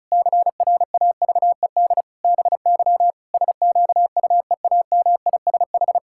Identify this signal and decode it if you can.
Morse code, heard as a keyed tone, about 36 wpm, 700 Hz. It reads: CRAVED BY SQUEAMISH
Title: Pangram Challenge v2 - 35wpm